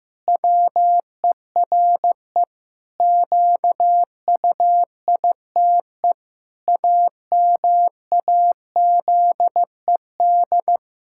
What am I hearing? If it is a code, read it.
Morse code, heard as a keyed tone, about 15 wpm, 700 Hz. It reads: WERE QUITE AMAZED
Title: Pangram Challenge v2 - 15wpm